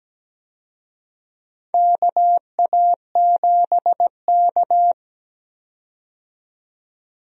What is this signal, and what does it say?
Morse code, heard as a keyed tone, about 17 wpm, 700 Hz. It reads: KA7K
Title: USA Call Signs - 4 Characters - 17wpm